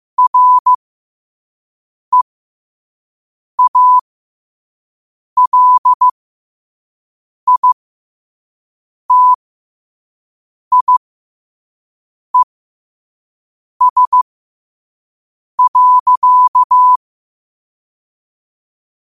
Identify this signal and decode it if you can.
Morse code, heard as a keyed tone, about 15 wpm, spaced as for 5 wpm, 1000 Hz. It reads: REALITIES.
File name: qotdfiles / 5wpm